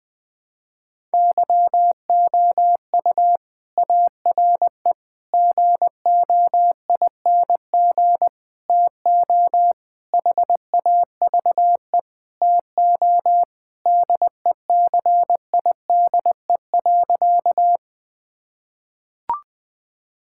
Morse code, heard as a keyed tone, about 20 wpm, 700 Hz. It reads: YOU ARE GOING TO HAVE TO DECIDE.
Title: Sentences with 4-Word Phrases - 20wpm